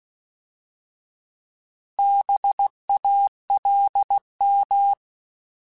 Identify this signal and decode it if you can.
Morse code, heard as a keyed tone, about 16 wpm, 800 Hz. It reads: BALM
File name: Alphabetical / Alphabetical mp3s / Bs @15